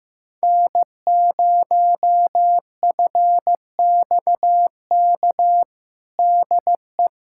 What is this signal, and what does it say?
Morse code, heard as a keyed tone, about 16 wpm, 700 Hz. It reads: N0FXK DE
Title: USA Callsign DE Callsign - 5 Characters - 15wpm